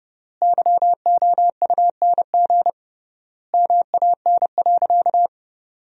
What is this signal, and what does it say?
Morse code, heard as a keyed tone, about 28 wpm, 700 Hz. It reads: YOUNG MAN.
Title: 3x Word-Spacing - Sentences from Top 500 Words - 30wpm